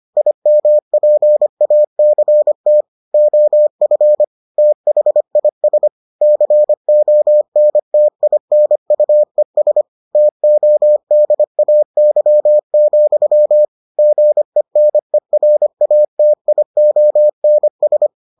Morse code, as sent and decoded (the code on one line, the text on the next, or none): .. -- .--. .- -.-. - / --- ..-. / - .... .. ... / -.-. --- -. - .. -. ..- . ... / - --- -.. .- -.-- --..-- / --. -. .-. .- - .. --- -. ...
IMPACT OF THIS CONTINUES TODAY, GENERATIONS